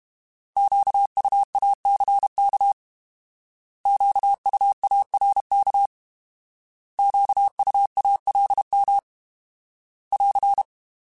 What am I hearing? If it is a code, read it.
Morse code, heard as a keyed tone, about 32 wpm, 800 Hz. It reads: QUACK QUARK QUALM <AR>
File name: Alphabetical / Alphabetical mp3s / Qs @30